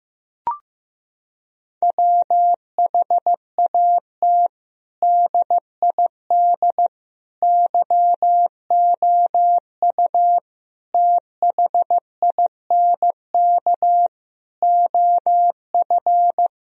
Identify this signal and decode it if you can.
Morse code, heard as a keyed tone, about 15 wpm, 700 Hz. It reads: WHAT DID YOU THINK OF
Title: Sentences from Top 1000 Words - Review Part 4 - 15wpm